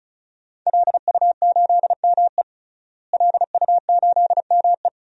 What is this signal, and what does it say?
Morse code, heard as a keyed tone, about 34 wpm, 700 Hz. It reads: LU8ME LU8ME
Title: DXCC-100 Call Signs - 5 Characters - Repeated 2x - 35wpm